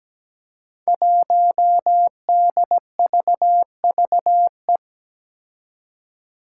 Morse code, heard as a keyed tone, about 17 wpm, 700 Hz. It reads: 1DVVE
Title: Five Challenging Letter-Number Combinations 17wpm